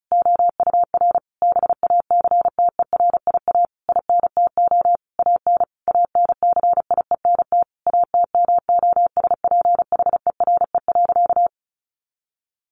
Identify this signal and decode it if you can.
Morse code, heard as a keyed tone, about 34 wpm, 700 Hz. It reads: OUR BACTERIA INTO AN ANCIENT ATMOSPHERE.
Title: A Sound Of Thunder 35wpm